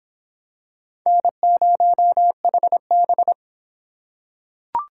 Morse code, heard as a keyed tone, about 26 wpm, 700 Hz. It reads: N0HB